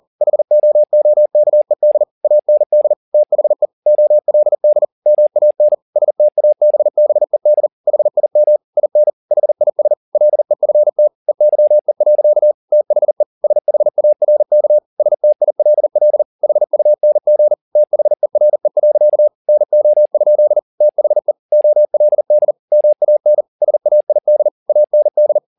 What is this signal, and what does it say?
Morse code, heard as a keyed tone, about 40 wpm, 600 Hz. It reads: HOOKED AND THE OLD MAN STABBED HIM IN HIS LEFT EYE. THE SHARK STILL HUNG THERE. NO? THE OLD MAN SAID AND
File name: the old man and the sea 2 40wpm